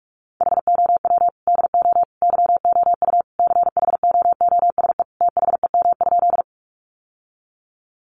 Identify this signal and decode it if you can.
Morse code, heard as a keyed tone, about 44 wpm, 700 Hz. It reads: HOW DO YOU CHOOSE THEM?
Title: Sentences from Top 700 Words 45wpm